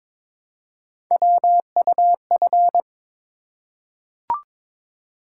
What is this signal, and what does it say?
Morse code, heard as a keyed tone, about 22 wpm, 700 Hz. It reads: WUF
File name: practice / 22wpm / Three Letter-Number Combinations - 22wpm